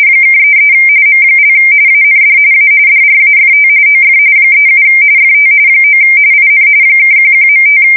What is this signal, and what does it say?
RTTY, about 45.45 baud. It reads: (33 M) AND A DRAFT OF 37 FT 9 IN (11.5
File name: fsk170 / USS-Missouri